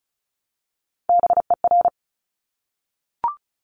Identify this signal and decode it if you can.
Morse code, heard as a keyed tone, about 36 wpm, 700 Hz. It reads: BER
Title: Top Three Letter Combinations in Words 35wpm